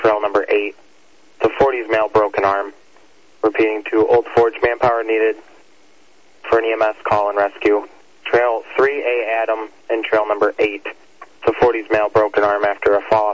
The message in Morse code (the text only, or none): none